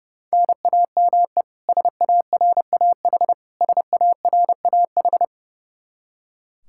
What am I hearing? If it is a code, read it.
Morse code, heard as a keyed tone, about 30 wpm, 700 Hz. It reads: NAME SARAH SARAH
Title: Practice QSO - Exchange 1 - Beginning - 30wpm